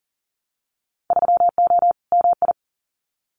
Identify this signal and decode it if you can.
Morse code, heard as a keyed tone, about 40 wpm, 700 Hz. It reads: 3O MI
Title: Winter Field Day Exchanges Encore 40wpm